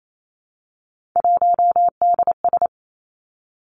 Morse code, heard as a keyed tone, about 28 wpm, 700 Hz. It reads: 1DS
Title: Three Challenging Letter-Number Combinations 28wpm